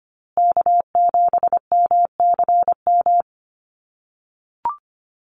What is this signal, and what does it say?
Morse code, heard as a keyed tone, about 25 wpm, 700 Hz. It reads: K7MCM
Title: USA Call Signs Repeated 3x 25wpm